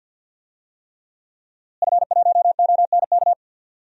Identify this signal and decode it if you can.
Morse code, heard as a keyed tone, about 48 wpm, 700 Hz. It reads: F1ONK
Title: DXCC-100 Call Signs - 5 Characters - Repeated 3x - 50wpm